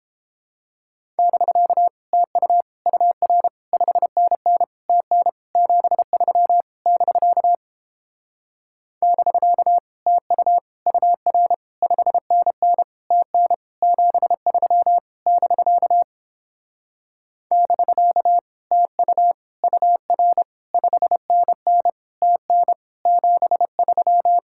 Morse code, heard as a keyed tone, about 31 wpm, 700 Hz. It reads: <BK> TU UR 5NN TN 73 <BK> <BK> TU UR 5NN TN 73 <BK> <BK> TU UR 5NN TN 73